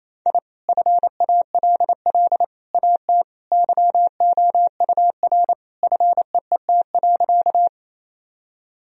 Morse code, heard as a keyed tone, about 28 wpm, 700 Hz. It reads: I FALL AT YOUR FEET.
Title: Sentences from Top 1000 Words - Review Part 2 - 28wpm